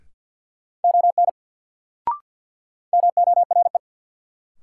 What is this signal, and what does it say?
Morse code, heard as a keyed tone, about 53 wpm, 700 Hz. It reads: ON MORE
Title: Sentences from Top 100 Words - New Word Drill - 50wpm